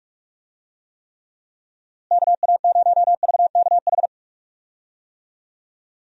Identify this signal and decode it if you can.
Morse code, heard as a keyed tone, about 45 wpm, 700 Hz. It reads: KA0VKH
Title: USA Call Signs - 6 Characters - Repeated 3x - 45wpm